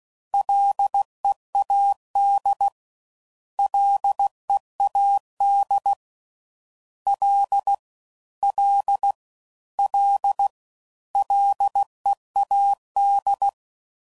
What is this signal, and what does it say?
Morse code, heard as a keyed tone, about 16 wpm, 800 Hz. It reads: LEAD LEAD L L L LEAD
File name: Prefixes / FLs & THs / Single Fs Ls